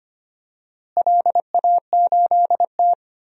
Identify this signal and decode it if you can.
Morse code, heard as a keyed tone, about 25 wpm, 700 Hz. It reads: LA8T
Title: DXCC-100 Call Signs - 4 Characters - Repeated 3x - 25wpm